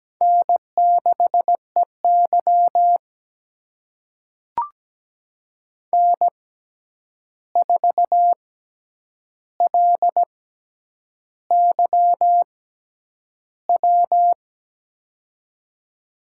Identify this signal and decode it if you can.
Morse code, heard as a keyed tone, about 17 wpm, spaced as for 8 wpm, 700 Hz. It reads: N6EY N4LYW
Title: USA Call Signs - 6x Character Spacing - 17wpm